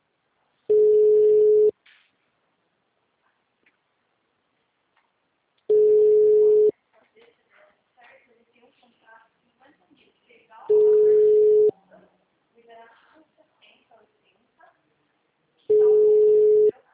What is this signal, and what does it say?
Ringback tone (Europe, 425 Hz), with speech.